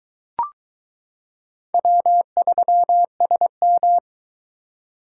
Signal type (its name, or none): Morse code